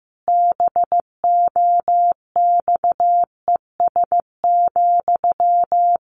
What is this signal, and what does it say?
Morse code, heard as a keyed tone, about 15 wpm, 700 Hz. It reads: BOXES,
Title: A Sound Of Thunder 15wpm